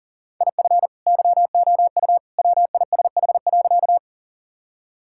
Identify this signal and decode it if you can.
Morse code, heard as a keyed tone, about 37 wpm, 700 Hz. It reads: IF YOU WISH.
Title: Sentences from Top 1600 Words 40wpm